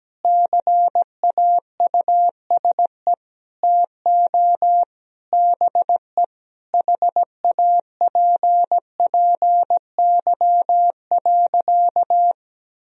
Morse code, heard as a keyed tone, about 17 wpm, 700 Hz. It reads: CAUSE TO BE HAPPY.
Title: Sentences from Top 1000 Words - Review Part 1 - 17wpm